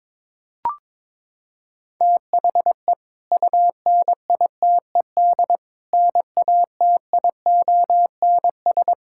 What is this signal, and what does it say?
Morse code, heard as a keyed tone, about 22 wpm, 700 Hz. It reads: THE UNITED NATIONS
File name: practice / 22wpm / Sentences with 2-Word Phrases - 22wpm